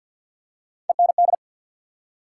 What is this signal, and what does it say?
Morse code, heard as a keyed tone, about 54 wpm, 700 Hz. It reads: END
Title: Three Letter Words - 50wpm